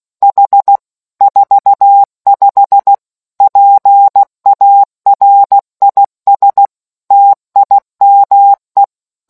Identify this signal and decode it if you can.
Morse code, heard as a keyed tone, about 16 wpm, 800 Hz. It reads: H 45 PARIS TIME